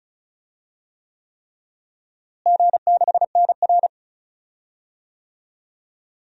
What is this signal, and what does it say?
Morse code, heard as a keyed tone, about 36 wpm, 690 Hz. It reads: G6NR
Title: DXCC-100 Call Signs - 4 Characters - Repeated 3x - 35wpm